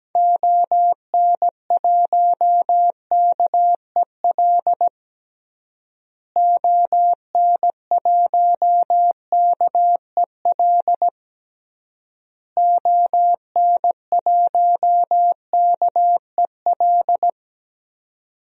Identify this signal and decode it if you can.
Morse code, heard as a keyed tone, about 17 wpm, 700 Hz. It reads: ON1KEL ON1KEL ON1KEL
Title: DXCC-100 Call Signs - 6 Characters - Repeated 3x - 17wpm